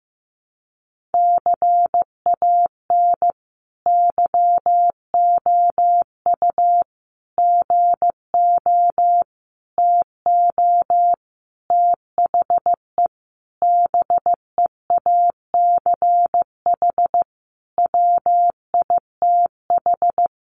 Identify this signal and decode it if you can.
Morse code, heard as a keyed tone, about 15 wpm, 700 Hz. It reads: CAN YOU GO TO THE BEACH WITH
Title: Sentences from Top 1500 Words 15wpm